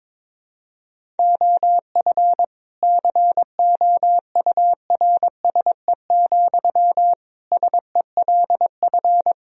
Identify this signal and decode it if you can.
Morse code, heard as a keyed tone, about 22 wpm, 700 Hz. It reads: OF COURSE, SELF
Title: Sentences from Top 1000 Words - Review Part 2 - 22wpm